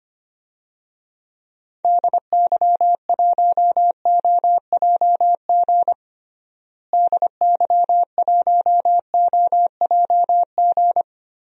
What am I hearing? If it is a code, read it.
Morse code, heard as a keyed tone, about 25 wpm, 700 Hz. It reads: DY1OJG DY1OJG
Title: DXCC-100 Call Signs - 6 Characters - Repeated 2x - 25wpm